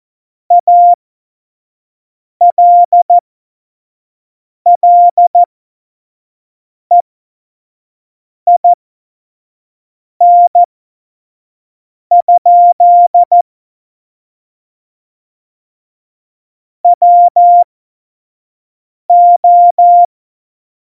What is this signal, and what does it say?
Morse code, heard as a keyed tone, about 14 wpm, spaced as for 5 wpm, 700 Hz. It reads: ALLEIN? WO
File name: Morsekurs / Witze / Witz 03.Juli 2018, 25 F70